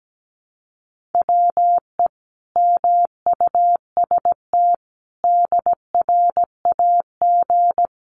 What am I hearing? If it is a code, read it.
Morse code, heard as a keyed tone, about 17 wpm, 700 Hz. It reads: WE MUST DRAG
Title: Sentences from Top 1500 Words 17wpm